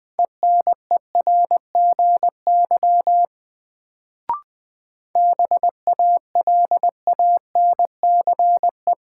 Morse code, heard as a keyed tone, about 20 wpm, 700 Hz. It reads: ENERGY BALANCE
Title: Sentences from Top 900 Words - New Word Drill - 20wpm